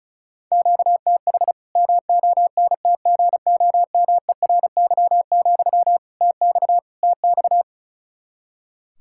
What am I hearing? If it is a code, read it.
Morse code, heard as a keyed tone, about 34 wpm, 700 Hz. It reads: QTH MONTGOMERY, TX TX
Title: Practice QSO - Exchange 1 - Beginning - 35wpm